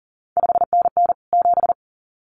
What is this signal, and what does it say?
Morse code, heard as a keyed tone, about 40 wpm, 700 Hz. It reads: 5NN 7
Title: CQ WW DX Contest Exchange Encore - 40wpm